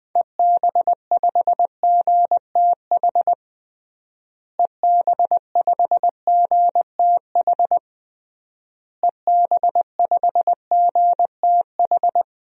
Morse code, heard as a keyed tone, about 20 wpm, 700 Hz. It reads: EB5GTH EB5GTH EB5GTH
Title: DXCC-100 Call Signs - 6 Characters - Repeated 3x - 20wpm